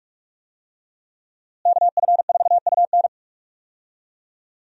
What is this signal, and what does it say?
Morse code, heard as a keyed tone, about 46 wpm, 700 Hz. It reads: KF4UN
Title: USA Call Signs - 5 Characters - Repeated 3x - 45wpm